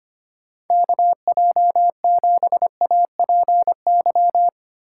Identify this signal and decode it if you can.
Morse code, heard as a keyed tone, about 25 wpm, 700 Hz. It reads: KJ7APY
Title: USA Call Signs - 8x Character Spacing - 25wpm